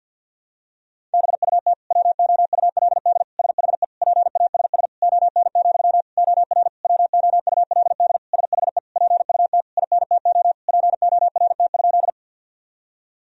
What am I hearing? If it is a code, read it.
Morse code, heard as a keyed tone, about 50 wpm, 700 Hz. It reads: BUT WOULD SHE PASS ON, OR WOULD SHE PUT INTO PORT?
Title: Sentences from Top 1000 Words - Review Part 1 - 50wpm